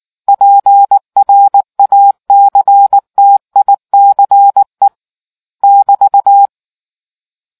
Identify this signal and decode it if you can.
Morse code, heard as a keyed tone, about 19 wpm, 800 Hz. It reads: PRACTICE <BT>